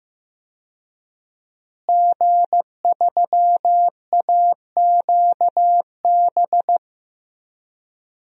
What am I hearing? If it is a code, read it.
Morse code, heard as a keyed tone, about 15 wpm, 700 Hz. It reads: G3AQB
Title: DXCC-100 Call Signs - 5 Characters - Repeated 2x - 15wpm